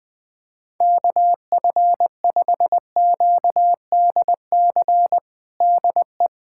Morse code, heard as a keyed tone, about 20 wpm, 700 Hz. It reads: KF5QDC DE
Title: USA Callsign DE Callsign 20wpm